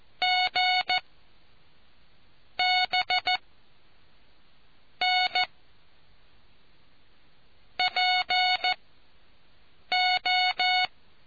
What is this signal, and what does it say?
Morse code, heard as a keyed tone, about 14 wpm, 730 Hz. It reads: G B N P O